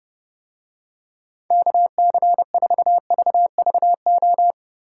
Morse code, heard as a keyed tone, about 30 wpm, 700 Hz. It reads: KC4VVO